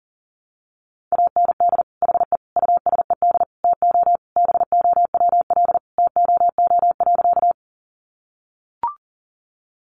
Morse code, heard as a keyed tone, about 40 wpm, 700 Hz. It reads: AND HE USED TO BOWL TOO.
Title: Sentences from Top 1200 Words 40wpm